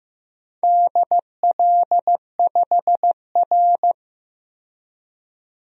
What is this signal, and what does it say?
Morse code, heard as a keyed tone, about 15 wpm, 700 Hz. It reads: DL5R
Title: DXCC-100 Call Signs - 4 Characters - 15wpm